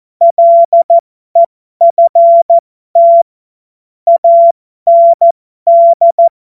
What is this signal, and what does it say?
Morse code, heard as a keyed tone, about 14 wpm, spaced as for 12 wpm, 680 Hz. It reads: LEFT AND
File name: Morsekurs / Witze